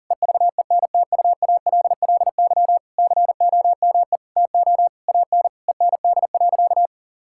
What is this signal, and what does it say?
Morse code, heard as a keyed tone, about 37 wpm, 700 Hz. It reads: EVENTUALLY COME TO AN END.
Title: Sentences from Top 1000 Words - Review Part 6 - 40wpm